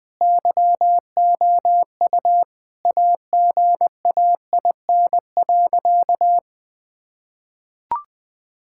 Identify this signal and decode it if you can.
Morse code, heard as a keyed tone, about 20 wpm, 700 Hz. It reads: YOU AGAIN.
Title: Sentences from Top 1600 Words 20wpm